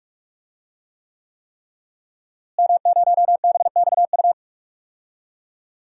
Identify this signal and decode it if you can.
Morse code, heard as a keyed tone, about 42 wpm, 700 Hz. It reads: M0BXU
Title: DXCC-100 Call Signs - 5 Characters - Repeated 3x - 45wpm